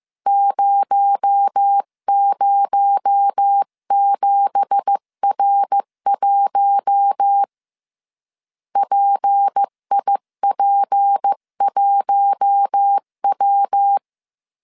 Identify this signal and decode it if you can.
Morse code, heard as a keyed tone, about 15 wpm, 800 Hz. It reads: 007R1 PIP1W
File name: CW / DET / 1 9 1500C S B 14wpm 3302